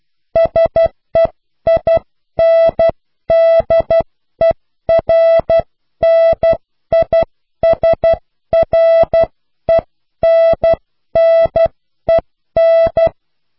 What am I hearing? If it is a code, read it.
Morse code, heard as a keyed tone, about 12 wpm, 660 Hz. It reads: SEINDERNISRENNEN
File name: cw / Pseudo woorden (15)